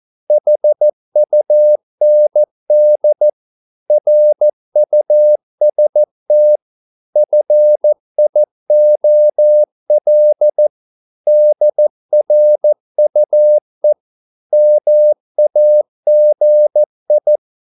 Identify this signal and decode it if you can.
Morse code, heard as a keyed tone, about 14 wpm, 600 Hz. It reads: HUND RUST FIOL DRUE MAGI
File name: Korte ord DK 14wpm